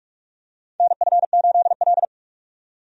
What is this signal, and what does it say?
Morse code, heard as a keyed tone, about 46 wpm, 700 Hz. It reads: NF8L
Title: USA Call Signs - 4 Characters - 45wpm